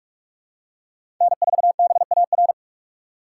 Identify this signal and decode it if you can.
Morse code, heard as a keyed tone, about 47 wpm, 700 Hz. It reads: N4BAR
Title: USA Call Signs - 4x Character Spacing - 45wpm